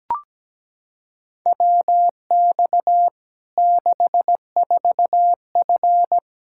Morse code, heard as a keyed tone, about 17 wpm, 700 Hz. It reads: WX 64F
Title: K1USN SST QSO Practice 17wpm